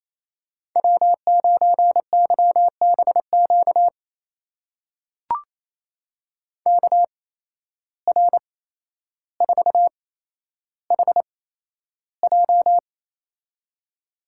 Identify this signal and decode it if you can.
Morse code, heard as a keyed tone, about 28 wpm, spaced as for 12 wpm, 700 Hz. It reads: W9YBQ KR4HJ
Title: USA Call Signs - 5 Characters - 8x Character Spacing - 28wpm